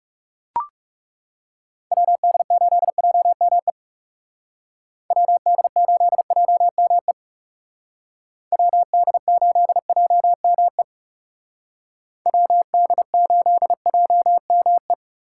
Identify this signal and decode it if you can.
Morse code, heard as a keyed tone, about 39 wpm, 700 Hz. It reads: WD8JME WD8JME WD8JME WD8JME